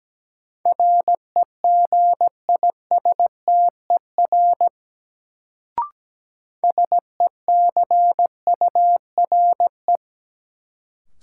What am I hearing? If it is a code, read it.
Morse code, heard as a keyed tone, about 17 wpm, 700 Hz. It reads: REGISTER SECURE
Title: Sentences from Top 1200 Words - New Word Drill - 17wpm